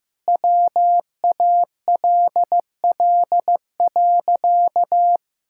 Morse code, heard as a keyed tone, about 16 wpm, 700 Hz. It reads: WALL.
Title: Sentences from Top 1000 Words - Review Part 6 - 15wpm